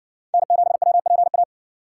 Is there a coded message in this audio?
Yes: Morse code from 0.3 to 1.4 s, heard as a keyed tone, about 57 wpm, 700 Hz. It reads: N7WPA